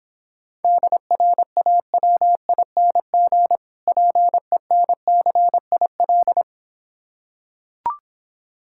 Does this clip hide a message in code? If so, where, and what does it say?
Yes: Morse code from 0.6 to 6.4 s, heard as a keyed tone, about 26 wpm, 700 Hz. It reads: DRAWING PENCIL